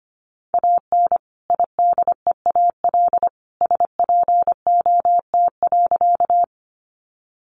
Morse code, heard as a keyed tone, about 25 wpm, 700 Hz. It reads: AN IDEAL SPOT.
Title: Sentences from Top 1400 Words 25wpm